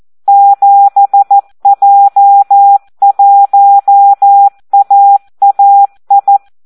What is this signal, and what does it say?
Morse code, heard as a keyed tone, about 14 wpm, 800 Hz. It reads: 7J1AAI
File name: Audio practica CW / SD 1 14